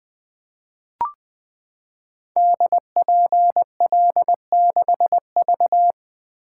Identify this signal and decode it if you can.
Morse code, heard as a keyed tone, about 20 wpm, 700 Hz. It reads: DPL6V
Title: Code Groups - 5 Letter:Numbers - 20wpm